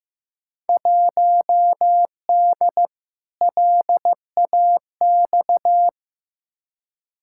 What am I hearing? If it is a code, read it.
Morse code, heard as a keyed tone, about 15 wpm, 700 Hz. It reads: 1D LAX
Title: Field Day Exchanges - 15wpm